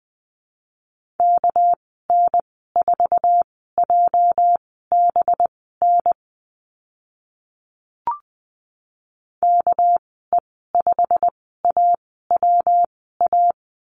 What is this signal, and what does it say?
Morse code, heard as a keyed tone, about 20 wpm, spaced as for 14 wpm, 700 Hz. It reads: KN4JBN KE5AWA
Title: 245 Lesson 39g - Letter z - Call Signs 20wpm